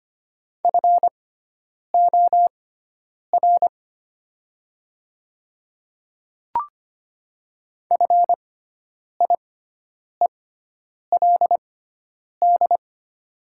Morse code, heard as a keyed tone, about 25 wpm, spaced as for 9 wpm, 700 Hz. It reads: FOR FIELD
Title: 6x Character Spacing - Top 500 Words - 25wpm